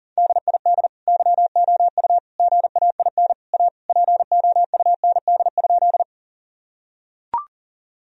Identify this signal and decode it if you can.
Morse code, heard as a keyed tone, about 40 wpm, 700 Hz. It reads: DID YOU GAIN A POUND?
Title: Sentences from Top 1000 Words - Review Part 2 - 40wpm